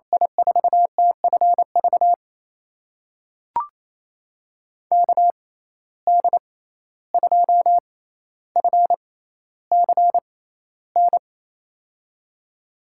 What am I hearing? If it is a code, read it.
Morse code, heard as a keyed tone, about 28 wpm, spaced as for 14 wpm, 700 Hz. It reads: I4TFV KD2FCN
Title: USA Call Signs - 6 Characters - 6x Character Spacing - 28wpm